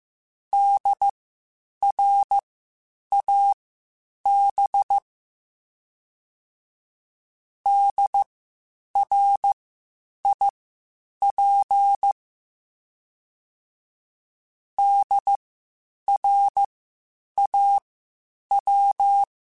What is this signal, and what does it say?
Morse code, heard as a keyed tone, about 15 wpm, spaced as for 8 wpm, 800 Hz. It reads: DRAB DRIP DRAW